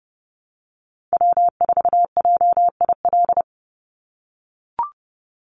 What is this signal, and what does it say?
Morse code, heard as a keyed tone, about 30 wpm, 700 Hz. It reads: W4JIL
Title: USA Call Signs 30wpm